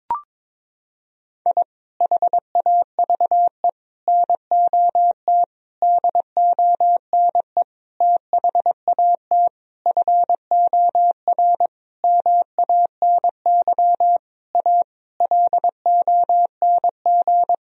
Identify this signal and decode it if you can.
Morse code, heard as a keyed tone, about 22 wpm, 700 Hz. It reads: I HAVE NOT DONE THAT FOR MANY A LONG
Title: Sentences from Top 1000 Words - Review Part 1 - 22wpm